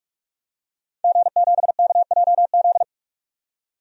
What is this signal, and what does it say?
Morse code, heard as a keyed tone, about 46 wpm, 700 Hz. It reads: G7KJZ
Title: DXCC-100 Call Signs - 5 Characters - 45wpm